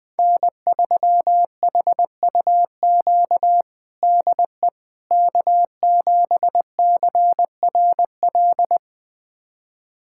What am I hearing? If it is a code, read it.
Morse code, heard as a keyed tone, about 21 wpm, 700 Hz. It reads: N3HUQ DE K7CRL
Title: USA Callsign DE Callsign - 5 Characters - 20wpm